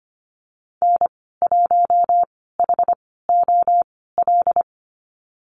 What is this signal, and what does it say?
Morse code, heard as a keyed tone, about 25 wpm, spaced as for 16 wpm, 700 Hz. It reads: N1HOL
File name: Lesson 15g - Number 5 - Call Signs 25wpm